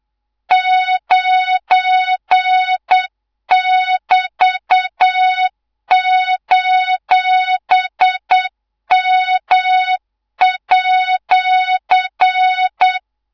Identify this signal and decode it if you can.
Morse code, heard as a keyed tone, about 8 wpm, 3700 Hz, with noise. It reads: 9<BT>:M@
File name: Morze-HA2MN / hangp / 2 03 GYAKORLAT